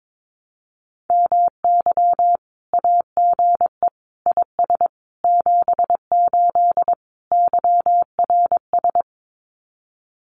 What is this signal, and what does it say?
Morse code, heard as a keyed tone, about 22 wpm, 700 Hz. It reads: MY AGE IS 78 YRS
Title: Common QSO Phrases Encore 22wpm